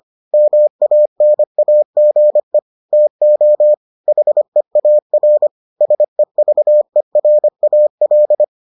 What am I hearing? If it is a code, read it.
Morse code, heard as a keyed tone, about 25 wpm, 600 Hz. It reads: MANAGE TO HEAR SEVERAL